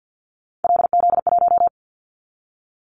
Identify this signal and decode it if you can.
Morse code, heard as a keyed tone, about 50 wpm, 690 Hz. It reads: LZ1
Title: Callsign Prefixes for Top 100 DXCC - 50wpm